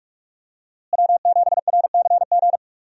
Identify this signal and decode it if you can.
Morse code, heard as a keyed tone, about 43 wpm, 700 Hz. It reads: W7RCG
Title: USA Call Signs - 5 Characters - Repeated 2x - 45wpm